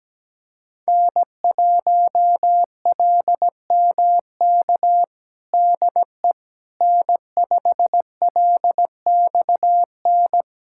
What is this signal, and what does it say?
Morse code, heard as a keyed tone, about 17 wpm, 700 Hz. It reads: N1LMK DE N5LXN
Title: USA Callsign DE Callsign - 5 Characters - 17wpm